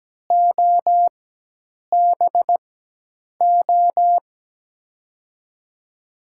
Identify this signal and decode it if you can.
Morse code, heard as a keyed tone, about 17 wpm, spaced as for 8 wpm, 700 Hz. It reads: OBO
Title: USA Call Signs - 4x Character Spacing - 17wpm